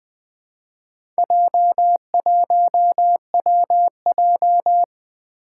Morse code, heard as a keyed tone, about 21 wpm, 700 Hz. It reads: J1WJ
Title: Four Challenging Letter-Number Combinations 20wpm